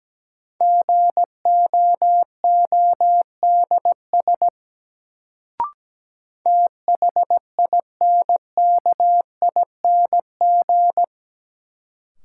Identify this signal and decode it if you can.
Morse code, heard as a keyed tone, about 17 wpm, 700 Hz. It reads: GOODS THINKING
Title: Sentences from Top 1000 Words - Review Part 6 - New Word Drill - 17wpm